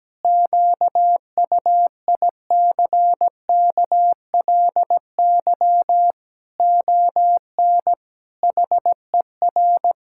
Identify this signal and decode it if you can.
Morse code, heard as a keyed tone, about 17 wpm, 700 Hz. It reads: QUICKLY ON HER
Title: Pangram Challenge v2 - 17wpm